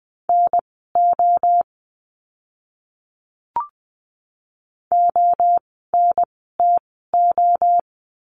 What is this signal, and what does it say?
Morse code, heard as a keyed tone, about 20 wpm, spaced as for 14 wpm, 700 Hz. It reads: NO ONTO
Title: 036 Lesson 7f - Letter s - Words 20wpm